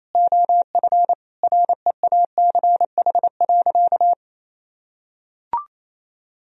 Morse code, heard as a keyed tone, about 27 wpm, 700 Hz. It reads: OF REACH.
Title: Sentences from Top 1000 Words - Review Part 2 - 28wpm